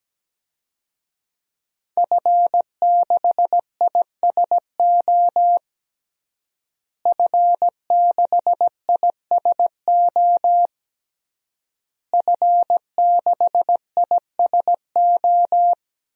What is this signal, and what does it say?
Morse code, heard as a keyed tone, about 17 wpm, 700 Hz. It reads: F6ISO F6ISO F6ISO
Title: DXCC-100 Call Signs - 5 Characters - Repeated 3x - 17wpm